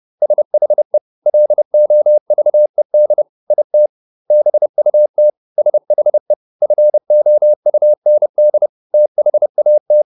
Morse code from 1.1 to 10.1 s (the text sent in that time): LOVED IT BUT SHE FOUND THAT